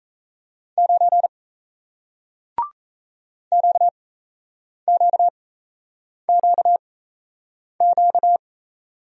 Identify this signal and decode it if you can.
Morse code, heard as a keyed tone, about 37 wpm, spaced as for 8 wpm, 700 Hz. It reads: 9 QQQQ